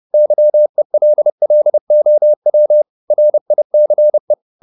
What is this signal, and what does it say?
Morse code, heard as a keyed tone, about 30 wpm, 600 Hz. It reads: YELLOW RICE
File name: the old man and the sea / the old man and the sea 1 30wpm